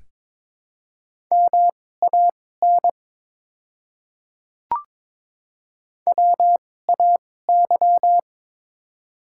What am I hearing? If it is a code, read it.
Morse code, heard as a keyed tone, about 22 wpm, spaced as for 16 wpm, 700 Hz. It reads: MAN WAY
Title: 2x Character Spacing - Top 300 Words - 22wpm